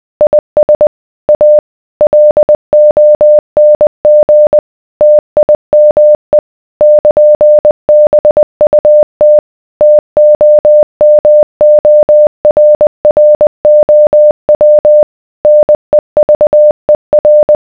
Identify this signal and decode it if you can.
Morse code, heard as a keyed tone, about 20 wpm, 600 Hz. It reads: IS A LONG TIME <KN>BUT TOMORROW NEVER